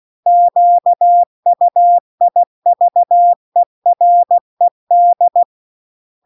Morse code, heard as a keyed tone, about 16 wpm, 700 Hz. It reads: QUIVERED
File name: CW-The-Big-Twoheaded-River-2-16WPM